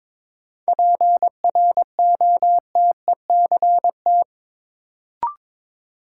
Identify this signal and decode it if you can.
Morse code, heard as a keyed tone, about 22 wpm, 700 Hz. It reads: PROTECT
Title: Sentences from Top 900 Words - New Word Drill - 22wpm